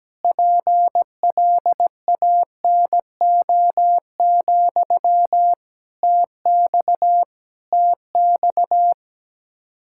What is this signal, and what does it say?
Morse code, heard as a keyed tone, about 17 wpm, 700 Hz. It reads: PLANO, TX TX
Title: Practice QSO - Exchange 1 - Beginning - 17wpm